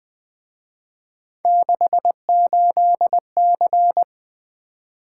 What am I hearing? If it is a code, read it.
Morse code, heard as a keyed tone, about 23 wpm, 700 Hz. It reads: 68C
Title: Speed-Racing - Three Letter-Number Combinations - 20wpm